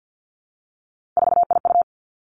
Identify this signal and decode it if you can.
Morse code, heard as a keyed tone, about 50 wpm, 700 Hz. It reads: 4IU
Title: Three Challenging Letter-Number Combinations 50wpm